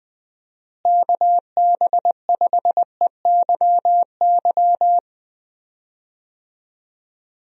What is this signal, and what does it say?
Morse code, heard as a keyed tone, about 20 wpm, 700 Hz. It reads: KB5EYY
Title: USA Call Signs - 6 Characters - Repeated 2x - 20wpm